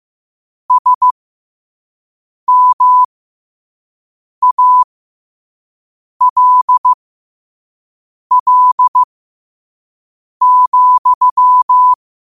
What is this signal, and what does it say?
Morse code, heard as a keyed tone, about 15 wpm, spaced as for 5 wpm, 1000 Hz. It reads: SMALL,
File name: qotdfiles / 5wpm